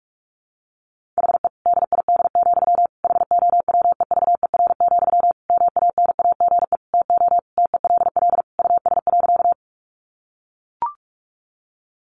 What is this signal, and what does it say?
Morse code, heard as a keyed tone, about 44 wpm, 700 Hz. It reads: HE DID, HOWEVER, MANAGE TO TELL US.
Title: Sentences from Top 1400 Words 45wpm